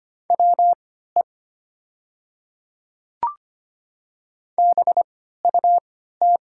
Morse code, heard as a keyed tone, about 25 wpm, spaced as for 14 wpm, 700 Hz. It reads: WE BUT
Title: 3x Character Spacing - Top 100 Words - 25wpm